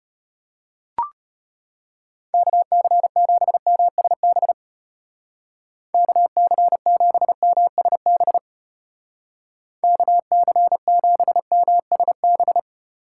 Morse code, heard as a keyed tone, about 30 wpm, 700 Hz. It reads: KC7MSB KC7MSB KC7MSB
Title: Speed Racing - USA 2x3 Call Signs 25wpm